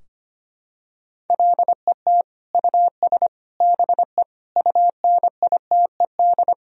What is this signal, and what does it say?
Morse code, heard as a keyed tone, about 25 wpm, 700 Hz. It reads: LET US BE UNITED
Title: Sentences from Top 1000 Words - Review Part 4 - 25wpm